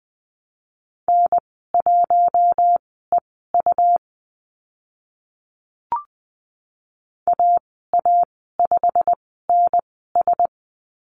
Morse code, heard as a keyed tone, about 20 wpm, spaced as for 15 wpm, 700 Hz. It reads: N1EU AA5NS
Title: 167 Lesson 27g - Number 7 - Call Signs 20wpm